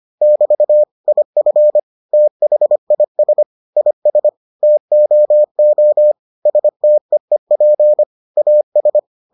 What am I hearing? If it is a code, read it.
Morse code, heard as a keyed tone, about 25 wpm, 600 Hz. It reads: <BT> IF THIS IS TOO STEEP AS